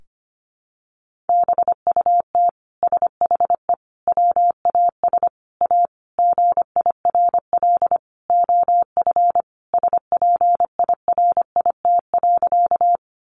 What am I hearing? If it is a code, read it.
Morse code, heard as a keyed tone, about 25 wpm, 700 Hz. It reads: BUT SHE WAS A GIRL OF SPIRIT.
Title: Sentences from Top 1400 Words 25wpm